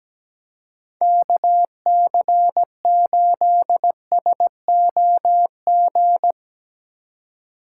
Morse code, heard as a keyed tone, about 17 wpm, 700 Hz. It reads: KC8SOG
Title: USA Call Signs - 6 Characters - 3x Character Spacing - 17wpm